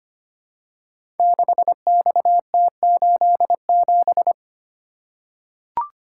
Morse code, heard as a keyed tone, about 25 wpm, 700 Hz. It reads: 6XT87
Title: Code Groups - 5 Letter:Numbers - 25wpm